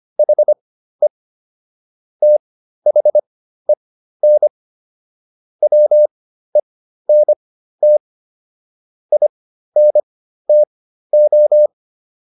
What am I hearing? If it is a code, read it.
Morse code, heard as a keyed tone, about 25 wpm, spaced as for 13 wpm, 600 Hz. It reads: HE THEN WENT INTO